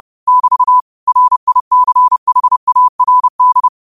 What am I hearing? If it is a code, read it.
Morse code, heard as a keyed tone, about 30 wpm, 1000 Hz. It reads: <BT> RICHARD